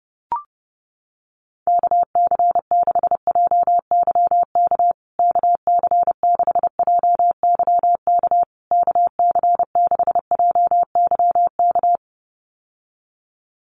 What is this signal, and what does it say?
Morse code, heard as a keyed tone, about 30 wpm, 700 Hz. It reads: KC6JYK KC6JYK KC6JYK
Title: USA Call Signs Repeated 3x 30wpm